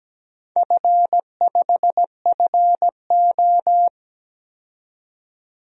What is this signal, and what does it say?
Morse code, heard as a keyed tone, about 17 wpm, 700 Hz. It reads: F5FO
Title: DXCC-100 Call Signs - 4 Characters - 17wpm